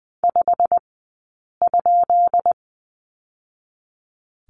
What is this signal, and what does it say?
Morse code, heard as a keyed tone, about 20 wpm, spaced as for 8 wpm, 700 Hz. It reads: 5?
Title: 151 Lesson 25d - Letter p - Two Characters 20wpm